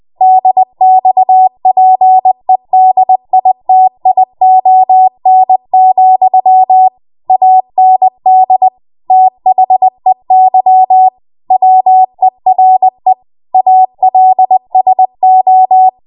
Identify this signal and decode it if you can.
Morse code, heard as a keyed tone, about 20 wpm, 750 Hz. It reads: DXPEDITION, AND THEY WERE ALSO
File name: Morse / 20WPM